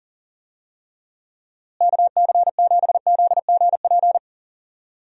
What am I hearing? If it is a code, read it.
Morse code, heard as a keyed tone, about 37 wpm, 700 Hz. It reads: KC7ZGP
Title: USA Call Signs - 6 Characters - Repeated 2x - 40wpm